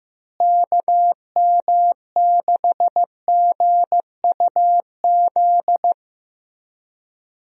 Morse code, heard as a keyed tone, about 15 wpm, 700 Hz. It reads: KM6GUZ
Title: USA Call Signs - 6 Characters - Repeated 2x - 15wpm